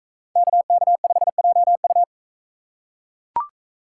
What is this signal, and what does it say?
Morse code, heard as a keyed tone, about 41 wpm, 700 Hz. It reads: KK5JU